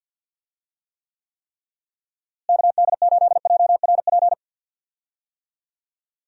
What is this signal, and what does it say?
Morse code, heard as a keyed tone, about 47 wpm, 700 Hz. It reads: KD8JRP
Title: USA Call Signs - 6 Characters - Repeated 3x - 50wpm